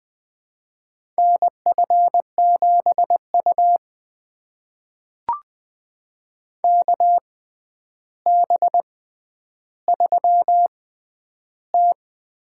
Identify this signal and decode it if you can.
Morse code, heard as a keyed tone, about 20 wpm, spaced as for 10 wpm, 700 Hz. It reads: NF7U KB3T
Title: USA Call Signs - 6x Character Spacing - 20wpm